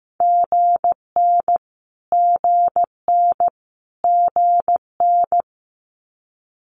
Morse code, heard as a keyed tone, about 15 wpm, 700 Hz. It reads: GN GN GN
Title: Multi-letter Phonemes Repeated 3x 15wpm